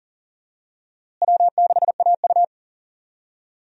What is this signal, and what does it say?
Morse code, heard as a keyed tone, about 40 wpm, 700 Hz. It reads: W6AU
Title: USA Call Signs - 4 Characters - Repeated 2x - 40wpm